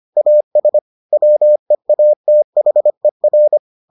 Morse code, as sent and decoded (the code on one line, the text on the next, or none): .- ... / .-- . .- - .... .-.
AS WEATHER